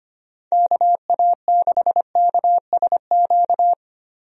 Morse code, heard as a keyed tone, about 25 wpm, 700 Hz. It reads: KA6KSQ
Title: USA Call Signs - 6 Characters - Repeated 3x - 25wpm